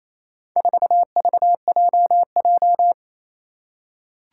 Morse code, heard as a keyed tone, about 28 wpm, 700 Hz. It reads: 4VJJ
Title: Four Challenging Letter-Number Combinations 28wpm